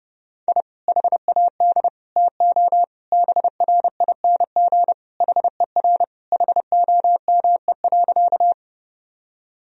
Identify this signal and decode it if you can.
Morse code, heard as a keyed tone, about 30 wpm, 700 Hz. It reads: I HAD TO BRING HER HOME.
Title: Sentences from Top 1000 Words - Review Part 3 - 30wpm